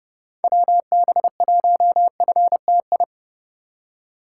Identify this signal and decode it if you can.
Morse code, heard as a keyed tone, about 30 wpm, 700 Hz. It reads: WB1FTI
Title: USA Call Signs - 8x Character Spacing - 30wpm